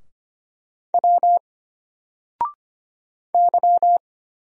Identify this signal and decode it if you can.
Morse code, heard as a keyed tone, about 25 wpm, 700 Hz. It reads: W Y
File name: Single-Letter-Number-25wpm